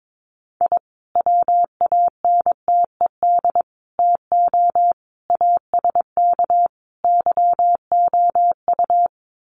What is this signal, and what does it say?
Morse code, heard as a keyed tone, about 22 wpm, 700 Hz. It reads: I WANTED TO ASK YOU
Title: Sets of 5 Words Encore2 22wpm